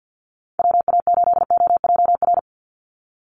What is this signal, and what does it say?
Morse code, heard as a keyed tone, about 50 wpm, 700 Hz. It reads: WA8OJR
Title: Call Signs Repeated 2x - 50wpm